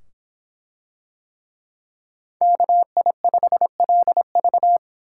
Morse code, heard as a keyed tone, about 25 wpm, 700 Hz. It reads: KI5LV